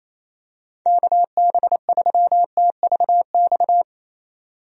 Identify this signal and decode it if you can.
Morse code, heard as a keyed tone, about 28 wpm, 700 Hz. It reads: KB3TVX
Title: USA Call Signs - 6 Characters - 28wpm